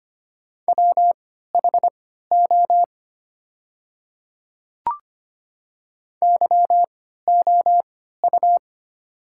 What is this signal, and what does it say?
Morse code, heard as a keyed tone, about 25 wpm, spaced as for 14 wpm, 700 Hz. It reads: WHO YOU
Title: 3x Character Spacing - Top 100 Words - 25wpm